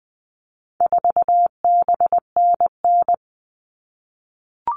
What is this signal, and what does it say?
Morse code, heard as a keyed tone, about 20 wpm, 700 Hz. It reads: 4BNN
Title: All Cut Numbers - 4-Digit 20wpm